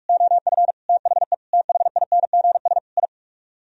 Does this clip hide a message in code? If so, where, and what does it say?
Yes: Morse code from 0.1 to 3.1 s, heard as a keyed tone, about 44 wpm, 700 Hz. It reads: OF THE THINGS I